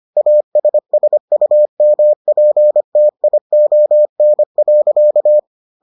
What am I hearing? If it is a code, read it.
Morse code, heard as a keyed tone, about 25 wpm, 600 Hz. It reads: ASSUMPTION.